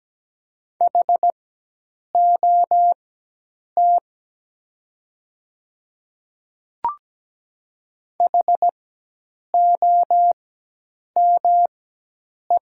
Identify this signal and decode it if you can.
Morse code, heard as a keyed tone, about 17 wpm, spaced as for 8 wpm, 690 Hz. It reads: HOT HOME
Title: 4x Character Spacing - Top 200 Words - 17wpm